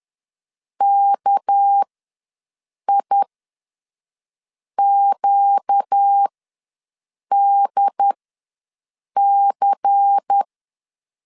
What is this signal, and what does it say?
Morse code, heard as a keyed tone, about 10 wpm, spaced as for 5 wpm, 800 Hz. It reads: KIQDC
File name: LEZIONI / 7 GF9Z / 7 G ws 10wpm 3843